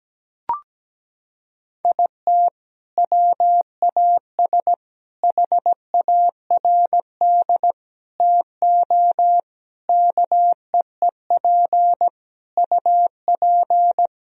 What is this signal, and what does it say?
Morse code, heard as a keyed tone, about 17 wpm, 700 Hz. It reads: IT WAS HARD TO KEEP UP
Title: Sentences from Top 1000 Words - Review Part 1 - 17wpm